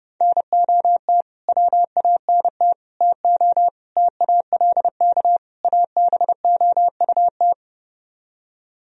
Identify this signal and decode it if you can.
Morse code, heard as a keyed tone, about 32 wpm, 700 Hz. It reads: NOT WANT TO TALK ABOUT